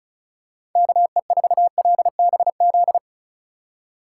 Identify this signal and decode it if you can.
Morse code, heard as a keyed tone, about 36 wpm, 700 Hz. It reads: KE4LBZ
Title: USA Call Signs - 6 Characters - 4x Character Spacing - 35wpm